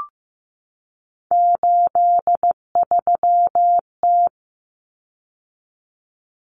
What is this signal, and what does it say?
Morse code, heard as a keyed tone, about 15 wpm, 700 Hz. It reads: 83T
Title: Common Cut Numbers - 3-Digit 15wpm